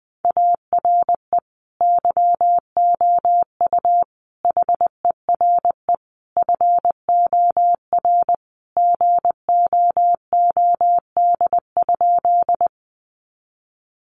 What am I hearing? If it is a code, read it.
Morse code, heard as a keyed tone, about 20 wpm, 700 Hz. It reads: ARE YOU HERE FOR GOOD?
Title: Sentences from Top 200 Words 20wpm